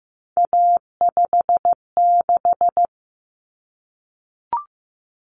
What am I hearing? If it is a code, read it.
Morse code, heard as a keyed tone, about 15 wpm, 700 Hz. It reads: A56
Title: Common Cut Numbers - 3-Digit 15wpm